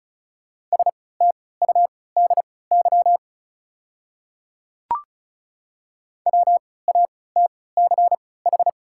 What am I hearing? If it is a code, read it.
Morse code, heard as a keyed tone, about 36 wpm, spaced as for 20 wpm, 700 Hz. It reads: STUDY WATCH
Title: 3x Character Spacing - Top 300 Words - 35wpm